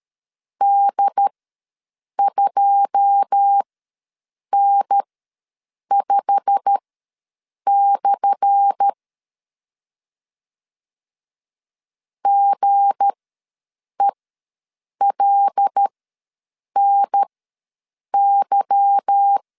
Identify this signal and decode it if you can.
Morse code, heard as a keyed tone, about 13 wpm, spaced as for 7 wpm, 800 Hz. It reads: D2N5/ GELNY
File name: LEZIONI / 10 SIGNS / 10A SLASH ws 12wpm 3443